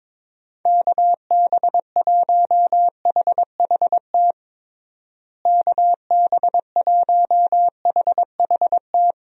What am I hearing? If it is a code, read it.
Morse code, heard as a keyed tone, about 22 wpm, 700 Hz. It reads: KB1HHT KB1HHT
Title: USA Call Signs - 6 Characters - Repeated 3x - 22wpm